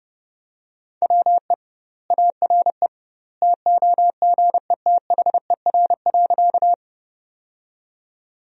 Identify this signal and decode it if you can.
Morse code, heard as a keyed tone, about 30 wpm, 700 Hz. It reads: WE ARE TOGETHER.
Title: 2x Word-Spacing - Sentences from Top 300 Words - 30wpm